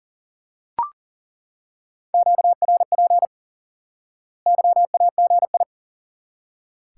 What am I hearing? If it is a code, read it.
Morse code, heard as a keyed tone, about 37 wpm, 700 Hz. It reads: QRP YAGI
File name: IQE Course 3 - Lesson 89 - YAGI - Two Elements - 40wpm